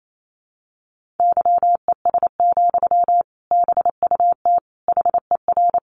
Morse code, heard as a keyed tone, about 28 wpm, 700 Hz. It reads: YES, BUT HER
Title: Sentences from Top 1200 Words 28wpm